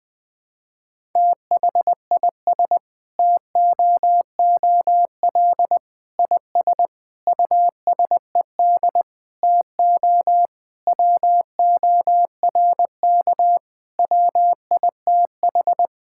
Morse code, heard as a keyed tone, about 20 wpm, 700 Hz. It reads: THIS TOOL IS USED TO WORK WITH
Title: Sentences from Top 1000 Words - Review Part 6 - 20wpm